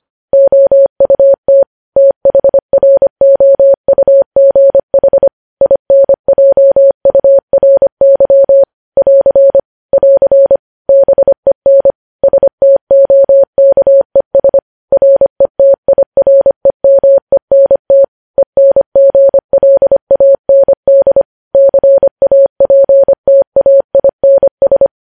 Morse code, heard as a keyed tone, about 25 wpm, 550 Hz. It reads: OUT THROUGH INJURY <AR> <AR> BEN STOKES RETIREMENT ENGLAND CAPTAINS
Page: news-headlines-in-morse-code-at-25-wpm